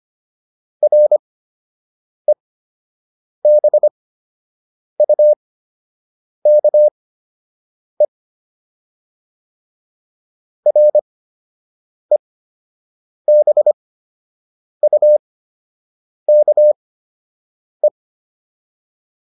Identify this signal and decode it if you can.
Morse code, heard as a keyed tone, about 25 wpm, spaced as for 7 wpm, 600 Hz. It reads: REBUKE REBUKE